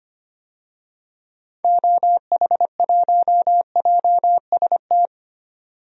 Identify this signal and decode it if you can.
Morse code, heard as a keyed tone, about 25 wpm, 700 Hz. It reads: OH1JST
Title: DXCC-100 Call Signs - 6 Characters - 25wpm